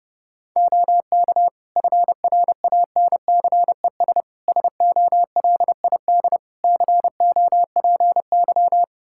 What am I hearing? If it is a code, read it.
Morse code, heard as a keyed tone, about 32 wpm, 700 Hz. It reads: OK FRANCES SOLID COPY
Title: Practice QSO - Exchange 3 - More Info - 30wpm